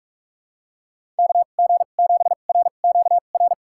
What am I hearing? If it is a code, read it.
Morse code, heard as a keyed tone, about 46 wpm, spaced as for 32 wpm, 700 Hz. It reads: KG7RQR
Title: USA Call Signs - 6 Characters - 2x Character Spacing - 45wpm